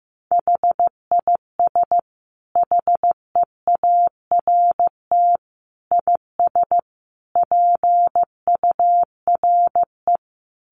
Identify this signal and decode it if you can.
Morse code, heard as a keyed tone, about 15 wpm, 700 Hz. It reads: HIS HEART IS PURE
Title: Sentences from Top 1500 Words 15wpm